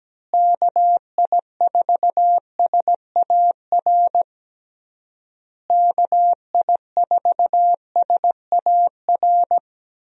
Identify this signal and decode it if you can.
Morse code, heard as a keyed tone, about 17 wpm, 700 Hz. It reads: KI4SAR KI4SAR
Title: USA Call Signs - 6 Characters - Repeated 3x - 17wpm